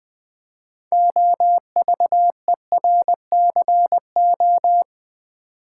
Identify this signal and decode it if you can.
Morse code, heard as a keyed tone, about 20 wpm, 700 Hz. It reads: OVERCO
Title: Build A Word - 3 Letters At A Time - 8-Letter Words - 20wpm